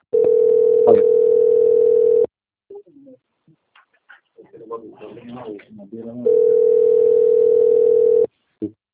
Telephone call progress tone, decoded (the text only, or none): ringback tone